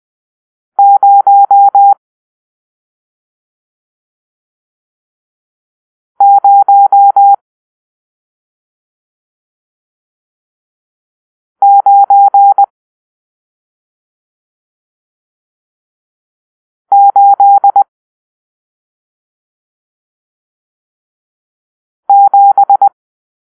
Morse code, heard as a keyed tone, about 20 wpm, 800 Hz. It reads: 0 0 9 8 7
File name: arrl / Grp-7 Drill-1